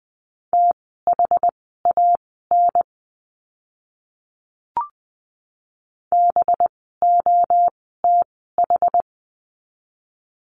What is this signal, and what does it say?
Morse code, heard as a keyed tone, about 20 wpm, spaced as for 14 wpm, 700 Hz. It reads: THAN BOTH
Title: 185 Lesson 30f - Letter b - Words 20wpm